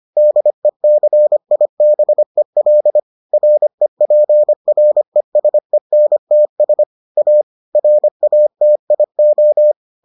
Morse code, heard as a keyed tone, about 25 wpm, 600 Hz. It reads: DECIBEL REPRESENTS A RATIO